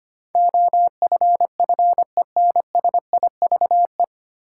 Morse code, heard as a keyed tone, about 25 wpm, 700 Hz. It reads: OFFENSIVE
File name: practice / 25wpm / 9 Letter Words - First 3 Letters Given - 25wpm